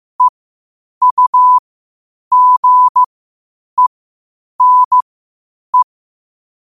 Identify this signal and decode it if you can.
Morse code, heard as a keyed tone, about 15 wpm, spaced as for 8 wpm, 1000 Hz. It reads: EUGENE
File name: qotdfiles / 8wpm